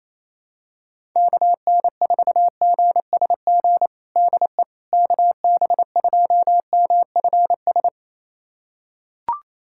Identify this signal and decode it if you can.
Morse code, heard as a keyed tone, about 28 wpm, 700 Hz. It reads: KN4GSG DE KB2MFS
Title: USA Callsign DE Callsign - 6 Characters - 28wpm